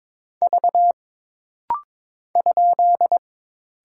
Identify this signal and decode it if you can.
Morse code, heard as a keyed tone, about 22 wpm, 700 Hz. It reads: V ?
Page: This rapid-fire timing is geared to push your ICR (Instant Character Recognition) skills.